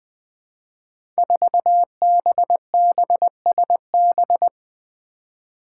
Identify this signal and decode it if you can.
Morse code, heard as a keyed tone, about 20 wpm, 700 Hz. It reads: 4BBSB
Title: Five Challenging Letter-Number Combinations 20wpm